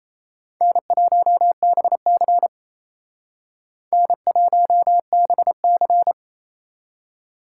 Morse code, heard as a keyed tone, about 36 wpm, 700 Hz. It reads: N1BC N1BC